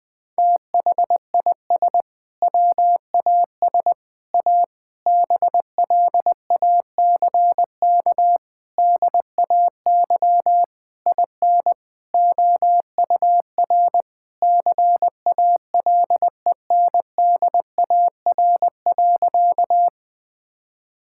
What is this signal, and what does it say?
Morse code, heard as a keyed tone, about 20 wpm, 700 Hz. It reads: THIS WAS A BLACK DAY IN OUR CALENDAR.
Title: Sentences from Top 1600 Words 20wpm